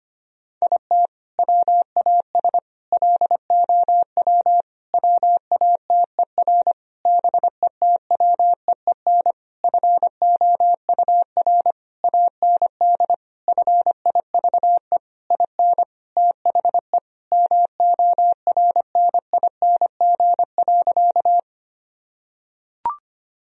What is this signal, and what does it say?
Morse code, heard as a keyed tone, about 25 wpm, 700 Hz. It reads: IT WAS LOW WATER BETWEEN FOUR AND FIVE IN THE MORNING.
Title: Sentences from Top 1000 Words - Review Part 2 - 25wpm